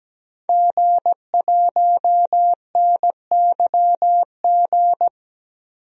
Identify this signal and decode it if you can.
Morse code, heard as a keyed tone, about 17 wpm, 700 Hz. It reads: G1NYG
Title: DXCC-100 Call Signs - 5 Characters - 17wpm